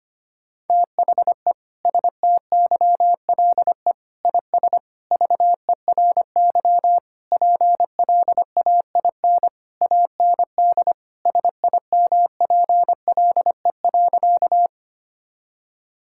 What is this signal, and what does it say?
Morse code, heard as a keyed tone, about 25 wpm, 700 Hz. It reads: THE STYLE IS VERY PLAIN AND SIMPLE.
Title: Sentences from Top 1000 Words - Review Part 5 - 25wpm